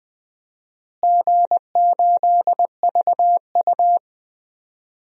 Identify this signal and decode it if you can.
Morse code, heard as a keyed tone, about 20 wpm, 700 Hz. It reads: G8VU
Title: DXCC-100 Call Signs - 4 Characters - Repeated 2x - 20wpm